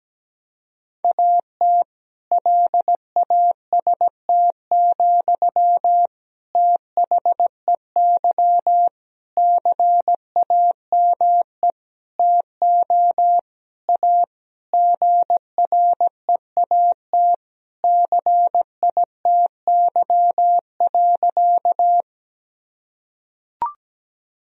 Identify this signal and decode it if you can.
Morse code, heard as a keyed tone, about 17 wpm, 700 Hz. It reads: AT LAST, THEY CAME TO A GREAT CITY.
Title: Sentences from Top 1000 Words - Review Part 1 - 17wpm